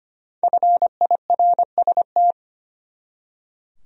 Morse code, heard as a keyed tone, about 25 wpm, 700 Hz. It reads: FIRST
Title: Top 100 Words 25wpm